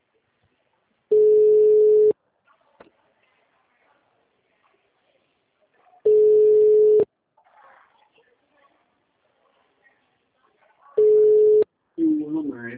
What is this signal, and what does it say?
Ringback tone (Europe, 425 Hz), with speech.